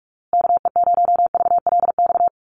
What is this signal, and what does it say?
Morse code, heard as a keyed tone, about 44 wpm, 700 Hz. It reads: KE0VLX
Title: USA Call Signs Repeated 3x 45wpm